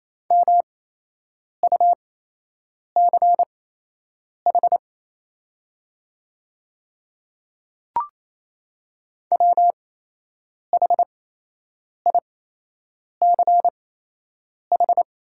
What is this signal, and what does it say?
Morse code, heard as a keyed tone, about 27 wpm, spaced as for 8 wpm, 700 Hz. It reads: MUCH WHICH
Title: 8x Character Spacing - Top 200 Words - 28wpm